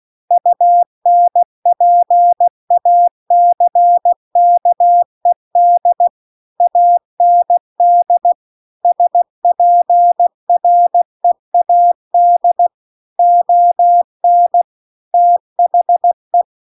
Morse code, heard as a keyed tone, about 16 wpm, 700 Hz. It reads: UNPACKED AND SPREAD ON THE